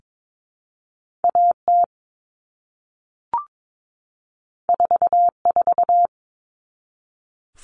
Morse code, heard as a keyed tone, about 22 wpm, 700 Hz. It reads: AT 44
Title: All Cut Numbers - 2-Digit 22wpm